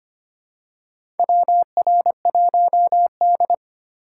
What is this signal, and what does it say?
Morse code, heard as a keyed tone, about 25 wpm, 700 Hz. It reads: WR1D
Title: USA Call Signs - 4 Characters - 2x Character Spacing - 25wpm